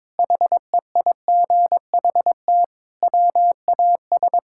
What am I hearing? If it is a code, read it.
Morse code, heard as a keyed tone, about 22 wpm, 700 Hz. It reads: HEIGHT WAS
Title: Sentences from Top 1600 Words 22wpm